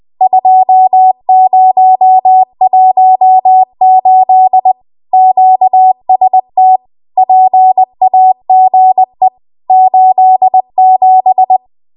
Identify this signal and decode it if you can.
Morse code, heard as a keyed tone, about 20 wpm, 750 Hz. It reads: 2018 QST PAGE 87
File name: Archive / 20 WPM / 20WPM